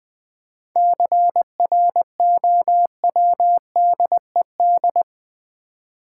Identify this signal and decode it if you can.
Morse code, heard as a keyed tone, about 21 wpm, 700 Hz. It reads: CROWDED
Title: 7 Letter Words - First 3 Letters Given - 20wpm